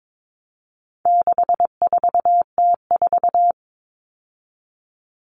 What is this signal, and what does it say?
Morse code, heard as a keyed tone, about 22 wpm, 700 Hz. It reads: 64T4
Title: All Cut Numbers - 4-Digit 22wpm